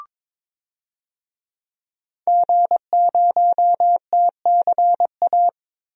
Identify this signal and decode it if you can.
Morse code, heard as a keyed tone, about 22 wpm, 700 Hz. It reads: G0TCA
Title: DXCC-100 Call Signs - 5 Characters - 22wpm